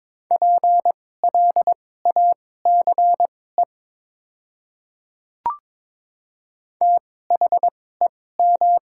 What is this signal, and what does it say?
Morse code, heard as a keyed tone, about 23 wpm, spaced as for 16 wpm, 690 Hz. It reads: PLACE THEM
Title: 2x Character Spacing - Top 200 Words - 22wpm